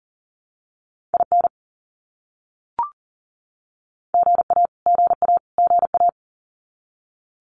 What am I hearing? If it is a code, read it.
Morse code, heard as a keyed tone, about 40 wpm, 700 Hz. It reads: IN GA GA GA
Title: US States Repeated 3x 40wpm